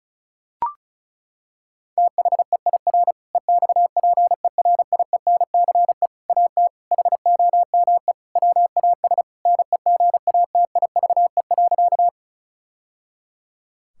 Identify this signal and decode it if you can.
Morse code, heard as a keyed tone, about 36 wpm, 700 Hz. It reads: THEIR EXPERIENCE AT HOME WAS NEGATIVE.
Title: Sentences from Top 1000 Words - Review Part 6 - 35wpm